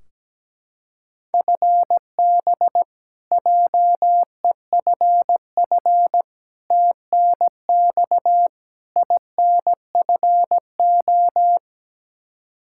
Morse code, heard as a keyed tone, about 17 wpm, 700 Hz. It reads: FB JEFF TNX INFO
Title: Practice QSO - Exchange 3 - More Info - 17wpm